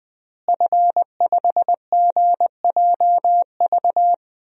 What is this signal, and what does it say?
Morse code, heard as a keyed tone, about 21 wpm, 700 Hz. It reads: F5GJV
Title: DXCC-100 Call Signs - 5 Characters - Repeated 3x - 20wpm